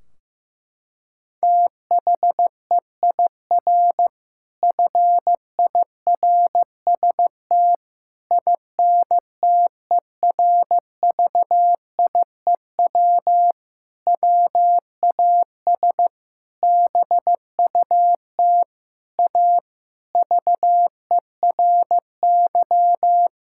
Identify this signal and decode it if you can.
Morse code, heard as a keyed tone, about 15 wpm, 700 Hz. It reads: THEIR FIRST INTERVIEW WAS BUT A VERY
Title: Sentences from Top 1600 Words 15wpm